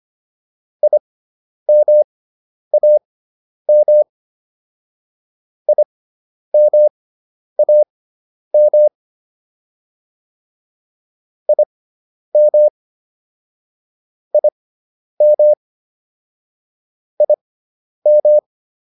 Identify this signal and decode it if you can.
Morse code, heard as a keyed tone, about 25 wpm, spaced as for 10 wpm, 600 Hz. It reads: IMAM IMAM IM IM IM
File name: IM4 10